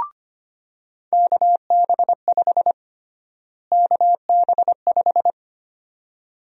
Morse code, heard as a keyed tone, about 25 wpm, 700 Hz. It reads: KB5 KB5
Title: USA Call Sign Prefixes - Repeated 2x - 25wpm